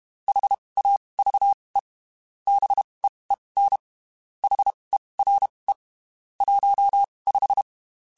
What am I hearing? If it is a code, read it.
Morse code, heard as a keyed tone, about 32 wpm, spaced as for 23 wpm, 800 Hz. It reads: HAVE BEEN HERE 15
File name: QSOs / QSOs-mp3s / WD0FBT 1@30